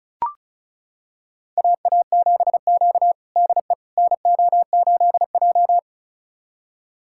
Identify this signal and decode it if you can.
Morse code, heard as a keyed tone, about 34 wpm, 700 Hz. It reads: AA7Q DE NO8J
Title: USA Callsign DE Callsign - 4 Characters - 35wpm